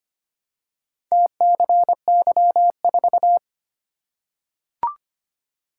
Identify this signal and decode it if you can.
Morse code, heard as a keyed tone, about 25 wpm, 700 Hz. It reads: TCY4
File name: practice / 25wpm / Four Letter-Number Combinations - 25wpm